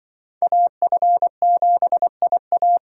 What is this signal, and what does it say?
Morse code, heard as a keyed tone, about 25 wpm, 700 Hz. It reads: AF7IA
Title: Speed Racing - USA Call Signs 20wpm